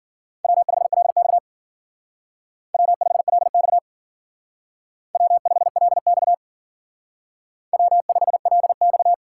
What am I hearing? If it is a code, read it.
Morse code, heard as a keyed tone, about 53 wpm, 700 Hz. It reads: W5LX W5LX W5LX W5LX